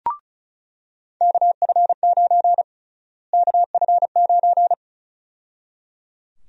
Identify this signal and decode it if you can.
Morse code, heard as a keyed tone, about 34 wpm, 700 Hz. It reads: KF9 KF9
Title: USA Call Sign Prefixes - Repeated 2x - 35wpm